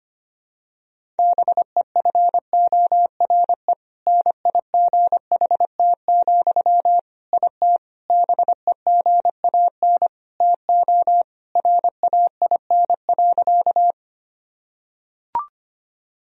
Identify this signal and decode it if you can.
Morse code, heard as a keyed tone, about 25 wpm, 700 Hz. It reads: BEFORE NIGHT, IT BEGAN TO RAIN.
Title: Sentences from Top 1000 Words - Review Part 1 - 25wpm